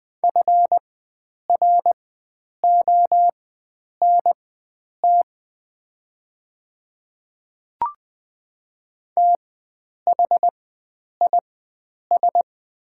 Morse code, heard as a keyed tone, about 20 wpm, spaced as for 9 wpm, 690 Hz. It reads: FRONT THIS
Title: 4x Character Spacing - Top 500 Words - 20wpm